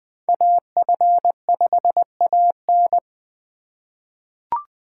Morse code, heard as a keyed tone, about 20 wpm, 700 Hz. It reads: AF5AN
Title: USA Call Signs - 3x Character Spacing - 20wpm